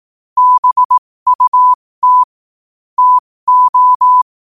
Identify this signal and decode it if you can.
Morse code, heard as a keyed tone, about 18 wpm, spaced as for 15 wpm, 1000 Hz. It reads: BUT TO
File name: qotdfiles / 15wpm